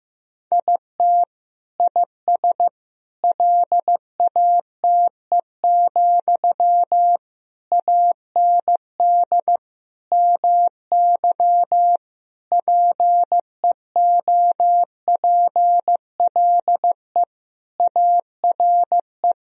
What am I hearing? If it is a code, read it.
Morse code, heard as a keyed tone, about 15 wpm, 700 Hz. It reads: IT IS LATE, AND MY PEOPLE ARE
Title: Sentences from Top 1000 Words - Review Part 3 - 15wpm